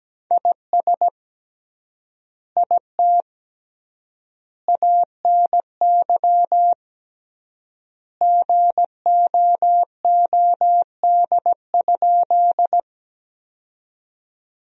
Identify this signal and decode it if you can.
Morse code, heard as a keyed tone, about 17 wpm, 700 Hz. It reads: IS IT ANY GOOD?
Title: 3x Word-Spacing - Sentences from Top 200 Words - 17wpm